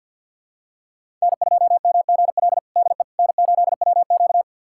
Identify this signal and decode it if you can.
Morse code, heard as a keyed tone, about 50 wpm, 700 Hz. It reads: N2MGL DE N8WQ
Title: USA Callsign DE Callsign 50wpm